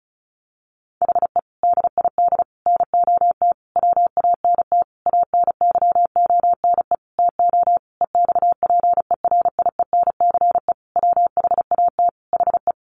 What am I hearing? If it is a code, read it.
Morse code, heard as a keyed tone, about 34 wpm, 700 Hz. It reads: HE DID NOT WANT ANYONE TO EXPERIENCE WHAT HE
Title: Sentences from Top 1100 Words 35wpm